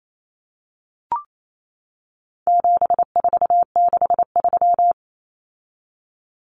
Morse code, heard as a keyed tone, about 28 wpm, 700 Hz. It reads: 7463
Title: Common Cut Numbers - 4-Digit 28wpm